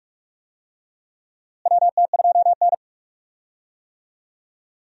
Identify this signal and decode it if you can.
Morse code, heard as a keyed tone, about 46 wpm, 700 Hz. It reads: WT2N
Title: USA Call Signs - 4 Characters - Repeated 2x - 45wpm